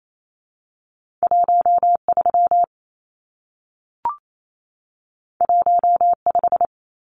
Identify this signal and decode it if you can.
Morse code, heard as a keyed tone, about 28 wpm, 700 Hz. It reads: 13 15
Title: Numbers - 2-Digit 28wpm